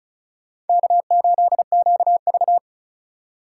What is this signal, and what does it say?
Morse code, heard as a keyed tone, about 45 wpm, 700 Hz. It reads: K8QV